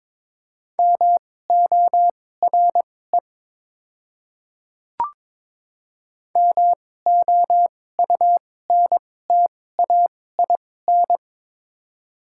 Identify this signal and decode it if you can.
Morse code, heard as a keyed tone, about 21 wpm, spaced as for 16 wpm, 700 Hz. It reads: MORE MOUNTAIN
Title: 2x Character Spacing - Top 300 Words - 22wpm